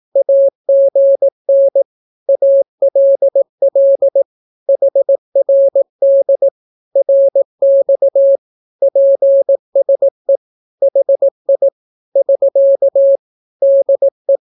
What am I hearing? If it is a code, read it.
Morse code, heard as a keyed tone, about 18 wpm, 550 Hz. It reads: AGN ALL HRD RX PSE HI <SK> DE
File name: naukapodstaw / skroty / skroty